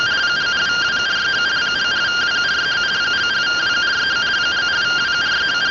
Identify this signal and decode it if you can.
Two-tone teleprinter signal, about 170 Hz shift.